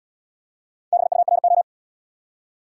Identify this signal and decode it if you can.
Morse code, heard as a keyed tone, about 68 wpm, 700 Hz. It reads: BURO